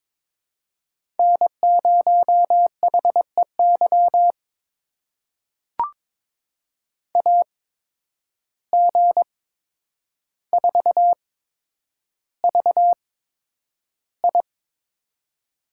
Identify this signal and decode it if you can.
Morse code, heard as a keyed tone, about 22 wpm, spaced as for 9 wpm, 700 Hz. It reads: N0HEY AG4VI
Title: USA Call Signs - 5 Characters - 8x Character Spacing - 22wpm